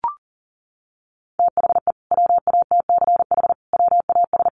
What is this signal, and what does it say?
Morse code, heard as a keyed tone, about 40 wpm, 700 Hz. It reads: THE WATCH WAS
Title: Sentences from Top 1300 Words 40wpm